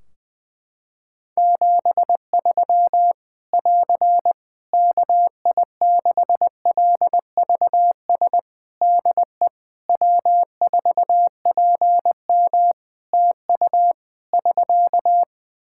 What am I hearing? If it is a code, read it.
Morse code, heard as a keyed tone, about 20 wpm, 700 Hz. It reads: 73 <AR> KI6LVS DE W4PM TU <SK>
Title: Practice QSO - Exchange 4 - Ending - 20wpm